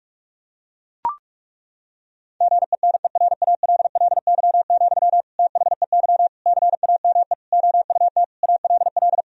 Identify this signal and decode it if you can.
Morse code, heard as a keyed tone, about 41 wpm, 700 Hz. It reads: GENERALLY, THEY CAME OUT ALL
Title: Sentences from Top 1000 Words - Review Part 4 - 45wpm